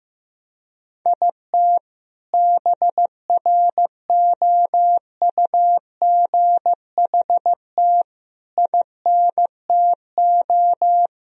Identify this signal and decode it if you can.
Morse code, heard as a keyed tone, about 16 wpm, 700 Hz. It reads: IT BROUGHT INTO
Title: Sentences from Top 1000 Words - Review Part 3 - 15wpm